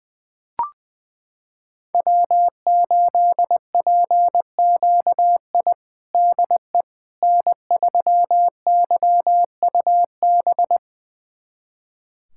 Morse code, heard as a keyed tone, about 20 wpm, 700 Hz. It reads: W8PQI DE N3YUB
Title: USA Callsign DE Callsign - 5 Characters - 20wpm